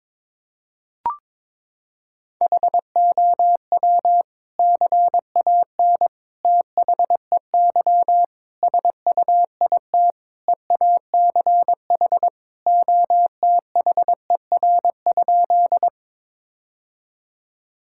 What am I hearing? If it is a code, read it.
Morse code, heard as a keyed tone, about 22 wpm, 700 Hz. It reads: HOW CAN THEY SUIT EACH OTHER?
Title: Sentences from Top 1600 Words 22wpm